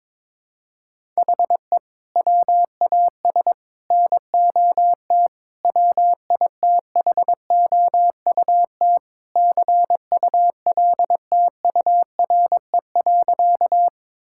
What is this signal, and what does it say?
Morse code, heard as a keyed tone, about 22 wpm, 700 Hz. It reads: HE WAS NOT WITHOUT CULTURE.
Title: Sentences from Top 1000 Words - Review Part 6 - 22wpm